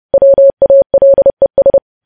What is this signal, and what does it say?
Morse code, heard as a keyed tone, about 30 wpm, 550 Hz. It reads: WALES
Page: news headlines in morse code at 30 wpm